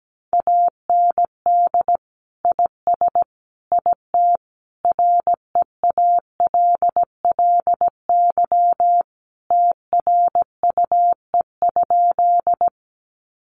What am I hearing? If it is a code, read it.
Morse code, heard as a keyed tone, about 17 wpm, 700 Hz. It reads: AND IS IT REALLY TRUE?
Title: Sentences from Top 600 Words 17wpm